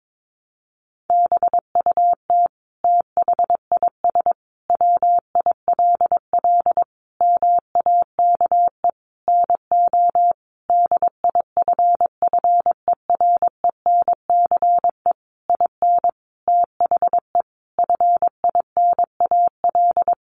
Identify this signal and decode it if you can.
Morse code, heard as a keyed tone, about 22 wpm, 700 Hz. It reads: BUT THIS WILL MAKE NO DIFFERENCE IN THE FINAL
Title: Sentences from Top 800 Words 22wpm